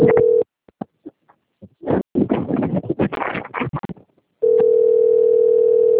Ringback tone (North America), 440 + 480 Hz.